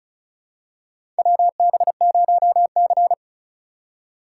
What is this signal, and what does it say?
Morse code, heard as a keyed tone, about 36 wpm, 700 Hz. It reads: WB0C
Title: USA Call Signs - 4 Characters - 3x Character Spacing - 35wpm